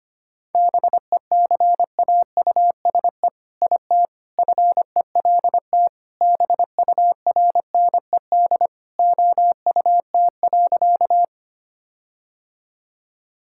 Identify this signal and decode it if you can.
Morse code, heard as a keyed tone, about 25 wpm, 700 Hz. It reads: BECAUSE IT FELT BURNED OUT.
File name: practice / 25wpm / All-Age Appropriate Jokes - 25wpm